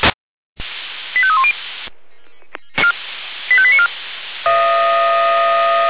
Five-tone selective call (ZVEI), heard as quick runs of five tones, tones of about 70 ms. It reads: 85310 85894